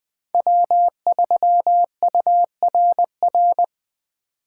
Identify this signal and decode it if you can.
Morse code, heard as a keyed tone, about 20 wpm, 700 Hz. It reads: W3URR
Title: Speed Racing - USA Call Signs 17wpm